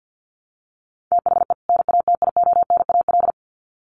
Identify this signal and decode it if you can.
Morse code, heard as a keyed tone, about 50 wpm, 700 Hz. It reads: THE NATIONAL
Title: Sets of 2 Words 50wpm